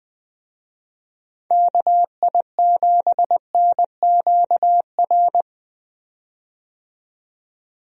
Morse code, heard as a keyed tone, about 20 wpm, 700 Hz. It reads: KI7NQR
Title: USA Call Signs - 6 Characters - 20wpm